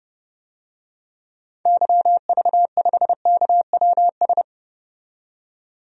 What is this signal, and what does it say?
Morse code, heard as a keyed tone, about 30 wpm, 700 Hz. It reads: YV5KWS
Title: DXCC-100 Call Signs - 6 Characters - 30wpm